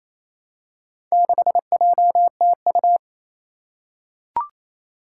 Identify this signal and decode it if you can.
Morse code, heard as a keyed tone, about 28 wpm, 700 Hz. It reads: 6JTU
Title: Four Challenging Letter-Number Combinations 28wpm